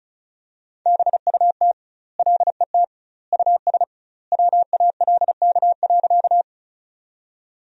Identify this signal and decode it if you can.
Morse code, heard as a keyed tone, about 34 wpm, 700 Hz. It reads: BUT LET US WALK.
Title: 2x Word-Spacing - Sentences from Top 300 Words - 35wpm